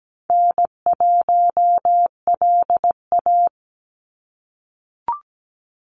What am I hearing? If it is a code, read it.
Morse code, heard as a keyed tone, about 17 wpm, 700 Hz. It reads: N1LA
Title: USA Call Signs 17wpm